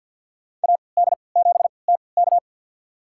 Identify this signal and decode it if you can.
Morse code, heard as a keyed tone, about 50 wpm, spaced as for 28 wpm, 700 Hz. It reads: AD7TK
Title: USA Call Signs - 5 Characters - 3x Character Spacing - 50wpm